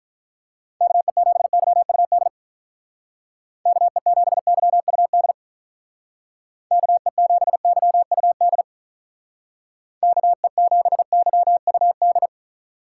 Morse code, heard as a keyed tone, about 45 wpm, 700 Hz. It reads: KE7YUD KE7YUD KE7YUD KE7YUD